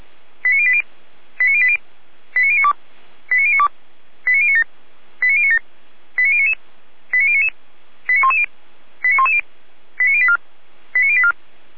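Five-tone selective call (ZVEI), heard as quick runs of five tones, tones of about 70 ms. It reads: 89080 89080 89092 89092 89097 89097 89099 89099 89110 89110 89084 89084
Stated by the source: ZVEI five-tone call received by SDR, recorded by Feuerwehr Orlamünde.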